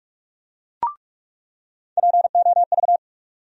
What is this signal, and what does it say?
Morse code, heard as a keyed tone, about 42 wpm, 700 Hz. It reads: POV
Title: USA Call Sign Postfixes - 3 Letters 45wpm